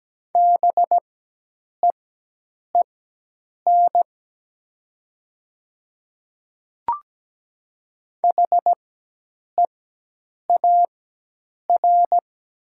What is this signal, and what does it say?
Morse code, heard as a keyed tone, about 17 wpm, spaced as for 8 wpm, 700 Hz. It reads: BEEN HEAR
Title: 4x Character Spacing - Top 500 Words - 17wpm